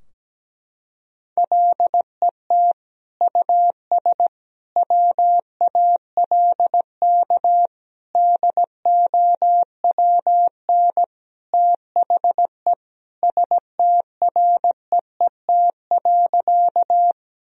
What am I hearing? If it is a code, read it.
Morse code, heard as a keyed tone, about 17 wpm, 700 Hz. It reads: LET US WALK DOWN THE STREET.
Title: Sentences from Top 1000 Words - Review Part 3 - 17wpm